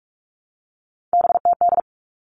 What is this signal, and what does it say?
Morse code, heard as a keyed tone, about 44 wpm, 700 Hz. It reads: BTD
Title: All Cut Numbers - 3-Digit 45wpm